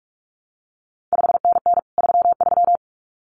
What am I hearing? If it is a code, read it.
Morse code, heard as a keyed tone, about 46 wpm, 700 Hz. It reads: 5NN 33
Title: CQ WW DX Contest Exchange Encore - 45wpm